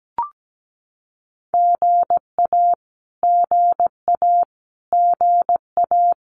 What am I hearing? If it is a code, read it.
Morse code, heard as a keyed tone, about 17 wpm, 700 Hz. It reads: GA GA GA
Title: Abbreviations Repeated 3x - 17wpm